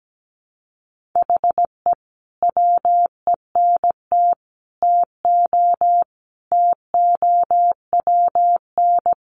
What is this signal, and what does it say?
Morse code, heard as a keyed tone, about 17 wpm, 700 Hz. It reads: HE WENT TO TOWN
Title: Sentences from Top 1300 Words 17wpm